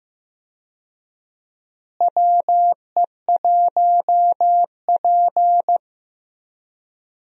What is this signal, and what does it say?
Morse code, heard as a keyed tone, about 15 wpm, 700 Hz. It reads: WE1P
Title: USA Call Signs - 4 Characters - Repeated 3x - 15wpm